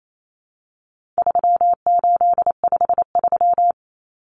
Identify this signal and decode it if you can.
Morse code, heard as a keyed tone, about 28 wpm, 700 Hz. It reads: 3853
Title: Common Cut Numbers - 4-Digit 28wpm